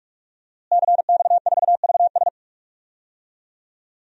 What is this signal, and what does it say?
Morse code, heard as a keyed tone, about 46 wpm, 700 Hz. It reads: CX4VS
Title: DXCC-100 Call Signs - 5 Characters - Repeated 3x - 45wpm